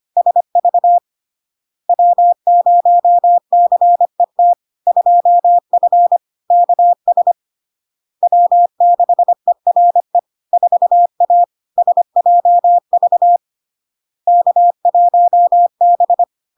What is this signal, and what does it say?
Morse code, heard as a keyed tone, about 25 wpm, 700 Hz. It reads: SV W0CET 2F KS W6ERE 4A SJV K1B
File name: LARK-FD-2022-Log-in-CW-25wpm-700-Hz